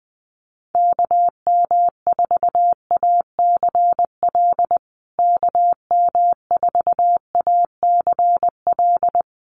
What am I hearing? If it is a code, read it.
Morse code, heard as a keyed tone, about 19 wpm, 700 Hz. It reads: KM4ACL KM4ACL
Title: USA Call Signs Repeated 3x 20wpm